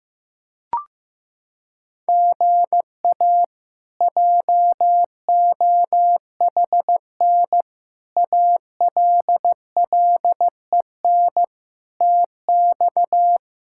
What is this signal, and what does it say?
Morse code, heard as a keyed tone, about 15 wpm, 700 Hz. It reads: GA JOHN ALLEN TX
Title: K1USN SST QSO Practice - 15wpm